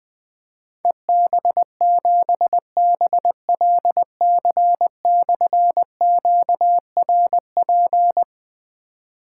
Morse code, heard as a keyed tone, about 20 wpm, 700 Hz. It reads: EB7BLC/QRP
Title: Difficult Call Signs 20wpm